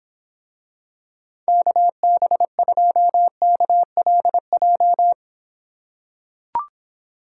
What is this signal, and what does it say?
Morse code, heard as a keyed tone, about 25 wpm, 700 Hz. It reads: KB2KLJ